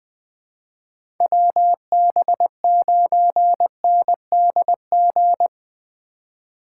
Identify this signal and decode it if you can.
Morse code, heard as a keyed tone, about 20 wpm, 700 Hz. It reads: WB9NDG
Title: 2023 POTA Hunter Call Signs - 20wpm